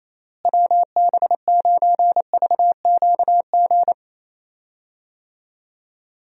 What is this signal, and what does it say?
Morse code, heard as a keyed tone, about 27 wpm, 700 Hz. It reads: WB9VQG
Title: USA Call Signs - 6 Characters - Repeated 2x - 28wpm